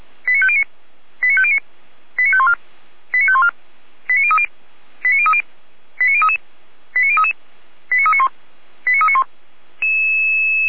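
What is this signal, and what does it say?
Five-tone selective call (ZVEI), heard as quick runs of five tones, tones of about 70 ms. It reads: 89509 89509 89514 89514 89030 89030 89033 89033 89391 89391
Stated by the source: ZVEI five-tone selective call 89042, received by SDR.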